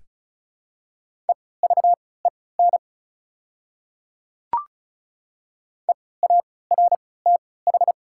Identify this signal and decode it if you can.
Morse code, heard as a keyed tone, about 34 wpm, spaced as for 20 wpm, 700 Hz. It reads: EVEN EARTH
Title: 3x Character Spacing - Top 300 Words - 35wpm